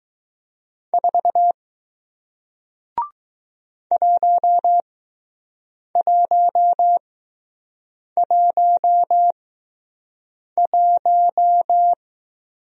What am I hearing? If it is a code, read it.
Morse code, heard as a keyed tone, about 20 wpm, spaced as for 6 wpm, 700 Hz. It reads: 4 1111